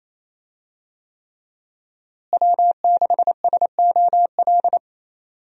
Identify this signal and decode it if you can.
Morse code, heard as a keyed tone, about 28 wpm, 700 Hz. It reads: W6SOL
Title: USA Call Signs - 5 Characters - Repeated 3x - 28wpm